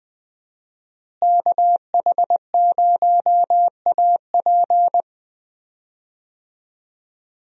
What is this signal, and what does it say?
Morse code, heard as a keyed tone, about 20 wpm, 700 Hz. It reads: KH0AP
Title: USA Call Signs - 5 Characters - Repeated 3x - 20wpm